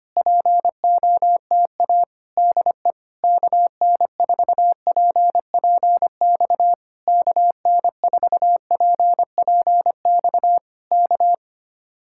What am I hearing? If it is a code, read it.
Morse code, heard as a keyed tone, about 25 wpm, 700 Hz. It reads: POTA DE KN4PPX KN4PPX K
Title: Speed-Racing Common QSO Phrases Encore - 25wpm